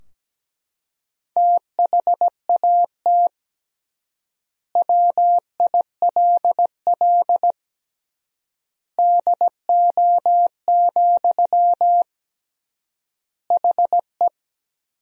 Morse code, heard as a keyed tone, about 17 wpm, 700 Hz. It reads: THAT WILL DO, HE
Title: 3x Word-Spacing - Sentences from Top 100 Words - 17wpm